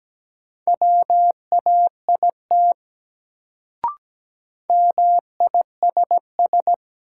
Morse code, heard as a keyed tone, about 17 wpm, 700 Hz. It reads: WAIT MISS
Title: Sentences from Top 500 Words - New Word Drill - 17wpm